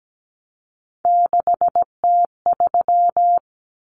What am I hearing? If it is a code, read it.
Morse code, heard as a keyed tone, about 17 wpm, 700 Hz. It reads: 6T3
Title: Common Cut Numbers - 3-Digit 17wpm